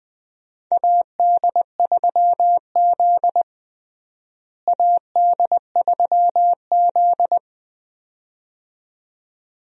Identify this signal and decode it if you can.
Morse code, heard as a keyed tone, about 20 wpm, 700 Hz. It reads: AD3Z AD3Z
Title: USA Call Signs - 4 Characters - Repeated 2x - 20wpm